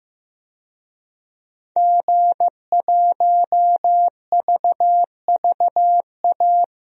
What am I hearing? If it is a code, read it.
Morse code, heard as a keyed tone, about 15 wpm, 700 Hz. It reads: G1VVA
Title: DXCC-100 Call Signs - 5 Characters - Repeated 2x - 15wpm